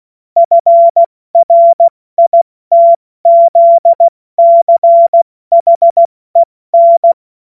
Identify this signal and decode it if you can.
Morse code, heard as a keyed tone, about 16 wpm, 680 Hz. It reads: FRITZCHEN
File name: Morsekurs / Witze / Witz 12. Oktober 2021, 70 F80